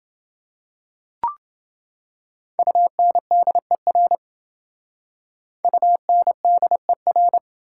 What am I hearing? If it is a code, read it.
Morse code, heard as a keyed tone, about 27 wpm, 700 Hz. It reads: UNDER UNDER